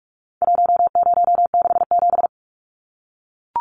Morse code, heard as a keyed tone, about 44 wpm, 700 Hz. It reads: 1067
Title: Numbers - 4-Digit 45wpm